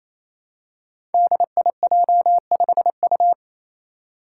Morse code, heard as a keyed tone, about 27 wpm, 700 Hz. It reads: DIJ5U
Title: Five Challenging Letter-Number Combinations 28wpm